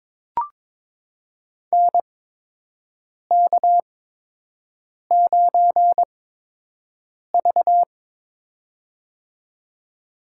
Morse code, heard as a keyed tone, about 21 wpm, spaced as for 6 wpm, 700 Hz. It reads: NK9V
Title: USA Call Signs - 4 Characters - 8x Character Spacing - 22wpm